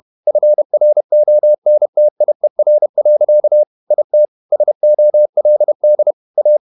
Morse code, heard as a keyed tone, about 31 wpm, 600 Hz. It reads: FRONTIER. IT SOLD A
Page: Never 31wpm | CW med Gnister